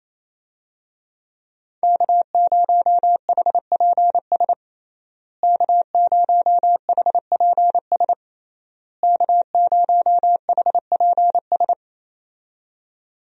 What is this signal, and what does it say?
Morse code, heard as a keyed tone, about 28 wpm, 700 Hz. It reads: K0HPS K0HPS K0HPS
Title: USA Call Signs - 5 Characters - Repeated 3x - 28wpm